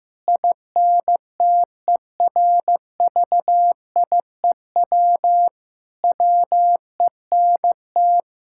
Morse code, heard as a keyed tone, about 15 wpm, 700 Hz. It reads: INTERVIEW WENT
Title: Sentences from Top 1600 Words 15wpm